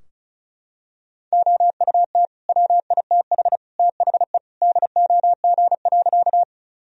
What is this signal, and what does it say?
Morse code, heard as a keyed tone, about 34 wpm, 700 Hz. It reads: OUT WITH THE DOG.
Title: Sentences from Top 1000 Words - Review Part 2 - 35wpm